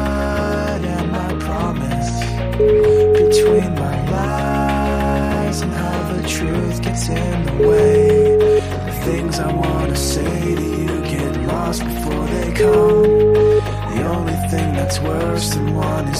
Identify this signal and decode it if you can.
Ringback tone (Europe, 425 Hz), with noise.